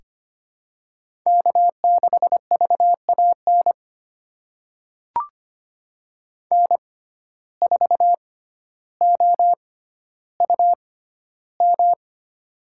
Morse code, heard as a keyed tone, about 25 wpm, spaced as for 13 wpm, 700 Hz. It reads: K6VAN N4OUM
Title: USA Call Signs - 5 Characters - 6x Character Spacing - 25wpm